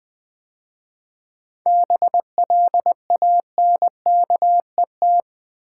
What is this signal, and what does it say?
Morse code, heard as a keyed tone, about 20 wpm, 700 Hz. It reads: BLANKET
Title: 7 Letter Words - First 3 Letters Given - 20wpm